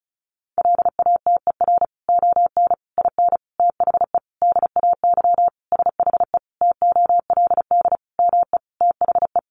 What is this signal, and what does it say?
Morse code, heard as a keyed tone, about 34 wpm, 700 Hz. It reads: LATER ON IN THE DAY SHE TOLD ME THE
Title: Sentences from Top 700 Words 35wpm